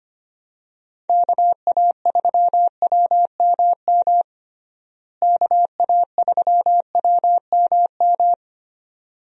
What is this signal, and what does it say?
Morse code, heard as a keyed tone, about 25 wpm, 700 Hz. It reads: KA3WMM KA3WMM
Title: USA Call Signs - 6 Characters - Repeated 2x - 25wpm